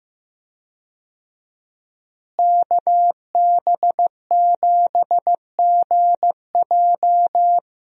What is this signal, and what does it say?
Morse code, heard as a keyed tone, about 15 wpm, 700 Hz. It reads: KB7GJ
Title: USA Call Signs - 5 Characters - Repeated 3x - 15wpm